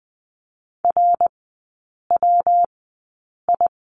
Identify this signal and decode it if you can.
Morse code, heard as a keyed tone, about 20 wpm, spaced as for 8 wpm, 700 Hz. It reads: RWI
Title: 133 Lesson 22e - Question Mark - Three Characters 20wpm